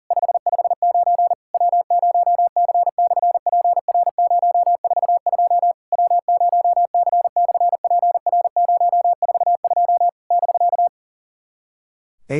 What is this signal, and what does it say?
Morse code, heard as a keyed tone, about 40 wpm, 700 Hz. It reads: 559 W0C/PR042 W0C/PR042 <BK>
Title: SOTA QSO Practice 40wpm